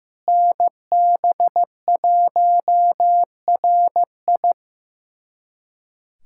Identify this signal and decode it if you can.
Morse code, heard as a keyed tone, about 15 wpm, 700 Hz. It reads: NB1RI
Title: USA Callsign DE Callsign 15wpm